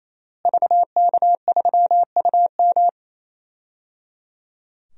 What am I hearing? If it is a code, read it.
Morse code, heard as a keyed tone, about 28 wpm, 700 Hz. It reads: VK3UM
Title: DXCC-100 Call Signs - 5 Characters - Repeated 3x - 28wpm